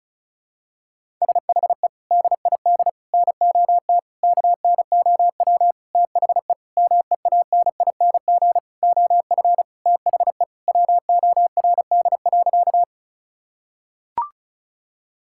Morse code, heard as a keyed tone, about 36 wpm, 700 Hz. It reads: SHE DID NOT KNOW THE MEANING OF THE WORD.
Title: Sentences from Top 1000 Words - Review Part 6 - 35wpm